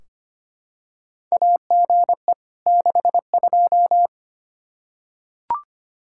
Morse code, heard as a keyed tone, about 25 wpm, 700 Hz. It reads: AGE 62
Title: Practice QSO - Exchange 3 - More Info - 25wpm